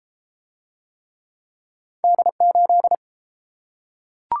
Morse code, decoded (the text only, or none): D8